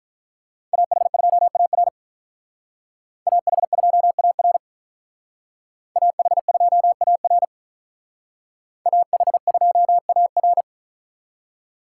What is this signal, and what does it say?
Morse code, heard as a keyed tone, about 45 wpm, 700 Hz. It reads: AH2AR AH2AR AH2AR AH2AR